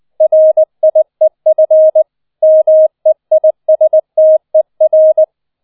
Morse code, heard as a keyed tone, about 20 wpm, 630 Hz. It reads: RIEF MEISTER